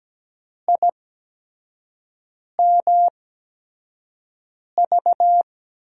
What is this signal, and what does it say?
Morse code, heard as a keyed tone, about 17 wpm, spaced as for 5 wpm, 700 Hz. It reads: IMV
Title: USA Call Signs - 6 Characters - 8x Character Spacing - 17wpm